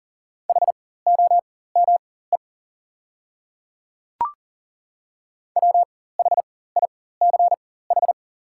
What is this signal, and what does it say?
Morse code, heard as a keyed tone, about 40 wpm, spaced as for 19 wpm, 690 Hz. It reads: HOME WHICH
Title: 4x Character Spacing - Top 500 Words - 40wpm